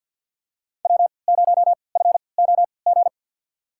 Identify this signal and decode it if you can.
Morse code, heard as a keyed tone, about 50 wpm, spaced as for 28 wpm, 700 Hz. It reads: W0FOG
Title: USA Call Signs - 5 Characters - 3x Character Spacing - 50wpm